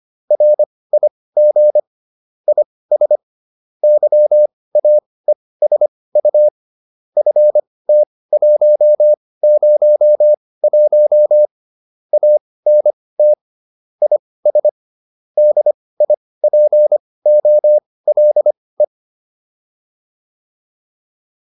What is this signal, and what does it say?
Morse code, heard as a keyed tone, about 25 wpm, spaced as for 18 wpm, 600 Hz. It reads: RIG IS YAESU FT101 ANT IS DIPOLE
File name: QSO405 18